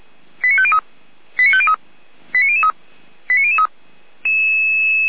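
Five-tone selective call (ZVEI), heard as quick runs of five tones, tones of about 70 ms. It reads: 89503 89503 89003 89003
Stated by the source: ZVEI five-tone selective call received with an SDR.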